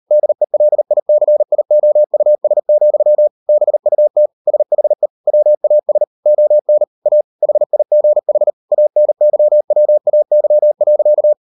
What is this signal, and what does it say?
Morse code, heard as a keyed tone, about 40 wpm, 600 Hz. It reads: DELICIOUS, BUT SHE WAS ON A HIGH ANYWAY.